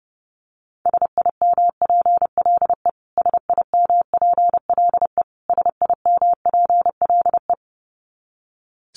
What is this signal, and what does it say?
Morse code, heard as a keyed tone, about 30 wpm, 700 Hz. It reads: SIMPLE SIMPLE SIMPLE
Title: Top 500 Words Repeated 30wpm